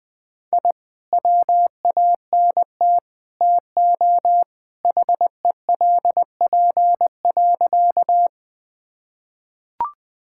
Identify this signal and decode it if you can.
Morse code, heard as a keyed tone, about 20 wpm, 700 Hz. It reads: I WANT TO HELP.
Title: Sentences with 3-Word Phrases - 20wpm